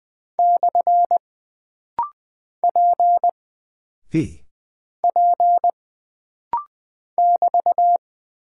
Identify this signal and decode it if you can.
Morse code, heard as a keyed tone, about 20 wpm, 700 Hz. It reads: / P P <BT>
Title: Single Character (Letter:Number:Punctuation:Prosign) - Rapid-Fire - 20wpm